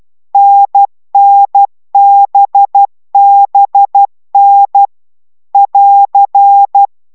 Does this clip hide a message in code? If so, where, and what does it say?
Yes: Morse code from 0.3 to 6.9 s, heard as a keyed tone, about 12 wpm, 800 Hz. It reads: NNBBN <AR>